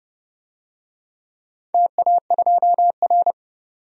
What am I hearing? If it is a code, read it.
Morse code, heard as a keyed tone, about 30 wpm, 700 Hz. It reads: TA2R
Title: DXCC-100 Call Signs - 4 Characters - 30wpm